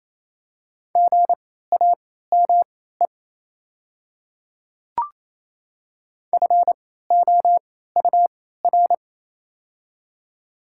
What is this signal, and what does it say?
Morse code, heard as a keyed tone, about 28 wpm, spaced as for 16 wpm, 700 Hz. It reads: GAME FOUR
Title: 3x Character Spacing - Top 500 Words - 28wpm